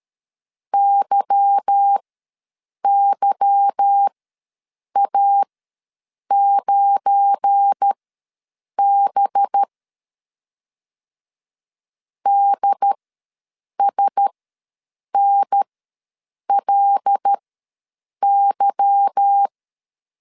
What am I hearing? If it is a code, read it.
Morse code, heard as a keyed tone, about 13 wpm, spaced as for 7 wpm, 800 Hz. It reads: YYA9B DSNLY
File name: LEZIONI / 8 YJ45 / 8 Y ws 12wpm 3409